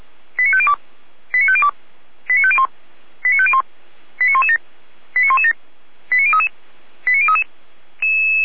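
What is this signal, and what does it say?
Five-tone selective call (ZVEI), heard as quick runs of five tones, tones of about 70 ms. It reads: 89502 89502 89501 89501 89117 89117 89033 89033
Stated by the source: ZVEI five-tone call received by SDR.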